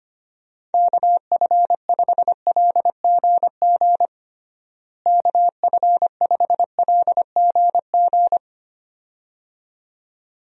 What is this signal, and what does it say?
Morse code, heard as a keyed tone, about 25 wpm, 700 Hz. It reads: KF5LGG KF5LGG
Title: USA Call Signs - 6 Characters - Repeated 2x - 25wpm